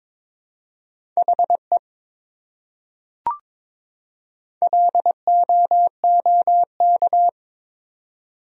Morse code, heard as a keyed tone, about 21 wpm, 700 Hz. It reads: HE LOOK
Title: Top 100 Words 22wpm